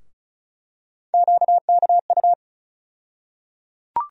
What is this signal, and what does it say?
Morse code, heard as a keyed tone, about 34 wpm, 700 Hz. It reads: QKU
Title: USA Call Sign Postfixes - 3 Letters 35wpm